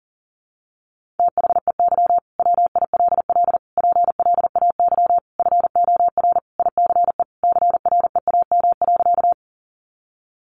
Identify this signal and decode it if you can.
Morse code, heard as a keyed tone, about 40 wpm, 700 Hz. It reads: THEY WILL PLAY FOR ICE CREAM.
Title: Sentences from Top 1500 Words 40wpm